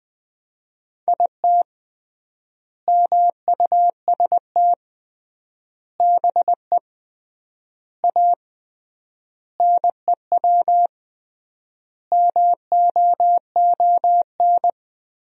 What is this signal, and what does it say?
Morse code, heard as a keyed tone, about 20 wpm, 700 Hz. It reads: IT MUST BE A NEW MOON
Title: 3x Word-Spacing - Sentences from Top 500 Words - 20wpm